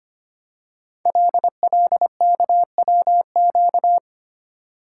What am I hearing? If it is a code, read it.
Morse code, heard as a keyed tone, about 25 wpm, 700 Hz. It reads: LLKWQ
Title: Code Groups - 5 Letter:Numbers - 25wpm